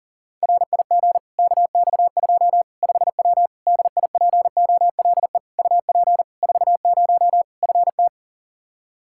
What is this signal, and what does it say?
Morse code, heard as a keyed tone, about 40 wpm, 700 Hz. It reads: RIG KX2 5W DIPOLE UP 40 FT
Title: Practice QSO - Exchange 2 - Info - 40wpm